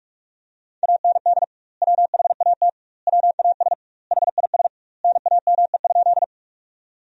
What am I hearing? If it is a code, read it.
Morse code, heard as a keyed tone, about 46 wpm, 700 Hz. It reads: AND WHAT WAS HIS NAME?
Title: 2x Word-Spacing - Sentences from Top 200 Words - 45wpm